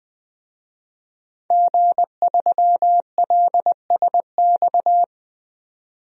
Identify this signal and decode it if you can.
Morse code, heard as a keyed tone, about 20 wpm, 700 Hz. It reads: G3LSX
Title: DXCC-100 Call Signs - 5 Characters - 20wpm